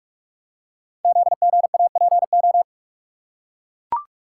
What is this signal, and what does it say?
Morse code, heard as a keyed tone, about 44 wpm, 690 Hz. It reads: ZGAPO
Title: Code Groups - 5 Letters - 45wpm